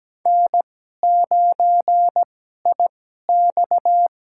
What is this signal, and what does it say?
Morse code, heard as a keyed tone, about 17 wpm, spaced as for 12 wpm, 700 Hz. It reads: N9IX
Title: USA Call Signs - 2x Character Spacing - 17wpm